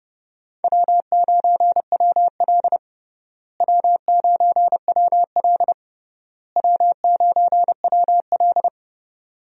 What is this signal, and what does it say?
Morse code, heard as a keyed tone, about 32 wpm, 700 Hz. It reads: W9WL W9WL W9WL
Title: USA Call Signs - 4 Characters - Repeated 3x - 30wpm